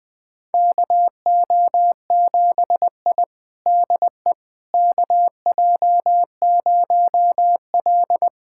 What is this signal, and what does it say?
Morse code, heard as a keyed tone, about 21 wpm, 700 Hz. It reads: KO7I DE KJ0L
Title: USA Callsign DE Callsign - 4 Characters - 20wpm